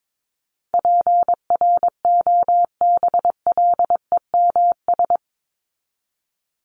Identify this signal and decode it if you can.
Morse code, heard as a keyed tone, about 22 wpm, 700 Hz. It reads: PROBLEMS
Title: ES:S Suffix - Top 100 Words with Base Words 22wpm